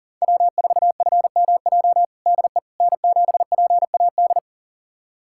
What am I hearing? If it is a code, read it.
Morse code, heard as a keyed tone, about 40 wpm, 700 Hz. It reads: W4FMJ DE N7PAD
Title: USA Callsign DE Callsign - 5 Characters - 40wpm